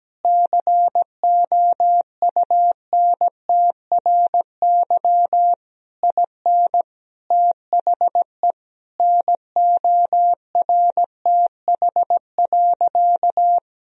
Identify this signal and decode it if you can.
Morse code, heard as a keyed tone, about 17 wpm, 700 Hz. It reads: COUNTRY IN THE NORTH.
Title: Sentences from Top 1000 Words - Review Part 1 - 17wpm